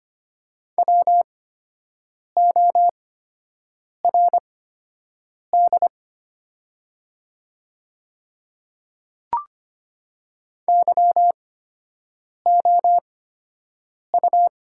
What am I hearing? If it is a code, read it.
Morse code, heard as a keyed tone, about 25 wpm, spaced as for 7 wpm, 700 Hz. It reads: WORD YOU
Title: 8x Character Spacing - Top 100 Words - 25wpm